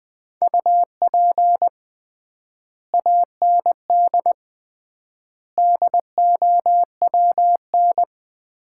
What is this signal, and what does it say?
Morse code, heard as a keyed tone, about 20 wpm, 700 Hz. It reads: UP AND DOWN
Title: 3x Word-Spacing - Sets of 4 Words - 20wpm